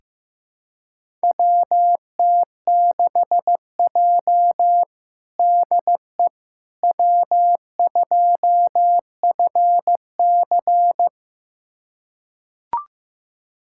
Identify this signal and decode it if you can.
Morse code, heard as a keyed tone, about 15 wpm, 700 Hz. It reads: WT6J DE W2FC
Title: USA Callsign DE Callsign - 4 Characters - 15wpm